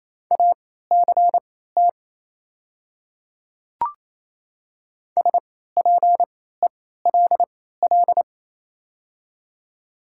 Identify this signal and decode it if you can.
Morse code, heard as a keyed tone, about 28 wpm, spaced as for 16 wpm, 700 Hz. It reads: ACT SPELL
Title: 3x Character Spacing - Top 500 Words - 28wpm